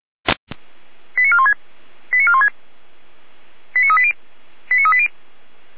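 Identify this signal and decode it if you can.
Five-tone selective call (ZVEI), heard as quick runs of five tones, tones of about 70 ms. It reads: 89416 89416 89380 89380
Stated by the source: ZVEI five-tone call 89402 received by SDR, channel 455.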